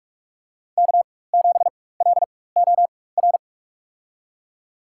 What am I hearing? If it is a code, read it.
Morse code, heard as a keyed tone, about 46 wpm, spaced as for 21 wpm, 700 Hz. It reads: K7LOR
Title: USA Call Signs - 4x Character Spacing - 45wpm